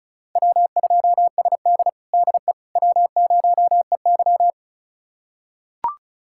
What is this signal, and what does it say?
Morse code, heard as a keyed tone, about 36 wpm, 700 Hz. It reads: W2SD DE W0EY
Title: USA Callsign DE Callsign - 4 Characters - 35wpm